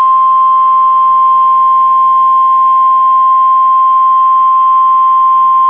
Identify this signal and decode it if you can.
Emergency Alert System (SAME) tones (the 1050 Hz attention tone). It is heard from 0.7 to 4.7 s.